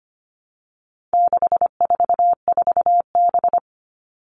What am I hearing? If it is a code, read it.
Morse code, heard as a keyed tone, about 25 wpm, 700 Hz. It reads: 644B
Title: All Cut Numbers - 4-Digit 25wpm